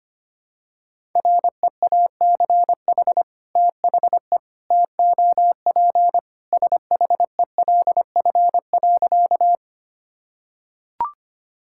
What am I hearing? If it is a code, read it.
Morse code, heard as a keyed tone, about 25 wpm, 700 Hz. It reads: REACH THE TOP SHELF.
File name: practice / 25wpm / Sentences with 2-Word Phrases - 25wpm